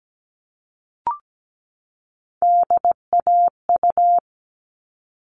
Morse code, heard as a keyed tone, about 17 wpm, 700 Hz. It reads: DAU
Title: All Cut Numbers - 3-Digit 17wpm